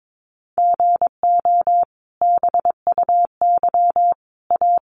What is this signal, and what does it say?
Morse code, heard as a keyed tone, about 22 wpm, 700 Hz. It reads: GO BUY A
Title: Sentences from Top 1200 Words 22wpm